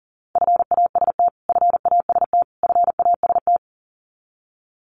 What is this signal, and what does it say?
Morse code, heard as a keyed tone, about 40 wpm, 700 Hz. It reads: FAST FAST FAST
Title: Top 500 Words Repeated 40wpm